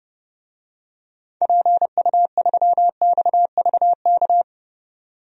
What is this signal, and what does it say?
Morse code, heard as a keyed tone, about 30 wpm, 700 Hz. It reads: PU3XVK
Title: DXCC-100 Call Signs - 6 Characters - 30wpm